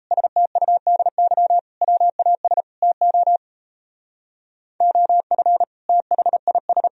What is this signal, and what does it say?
Morse code, heard as a keyed tone, about 33 wpm, 700 Hz. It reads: STUDY WAS TO OF THIS
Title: Speed-Racing - Sets of 5 Words 28wpm